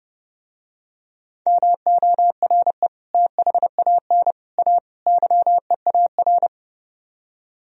Morse code, heard as a keyed tone, about 31 wpm, 700 Hz. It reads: MORE THAN A YEAR